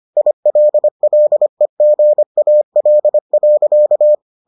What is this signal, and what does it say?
Morse code, heard as a keyed tone, about 25 wpm, 600 Hz. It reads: ILLEGAL.